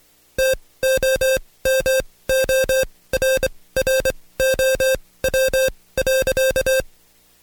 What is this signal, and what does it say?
Morse code, heard as a keyed tone, about 23 wpm, spaced as for 17 wpm, 530 Hz. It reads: TOMORROW.
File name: Code Quick / AudioDownld2 / Seg135-Stepper8 17WPM